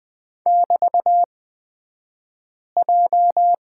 Morse code, heard as a keyed tone, about 20 wpm, spaced as for 5 wpm, 700 Hz. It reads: <BT>J
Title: 242 Lesson 39d - Letter z - Two Characters 20wpm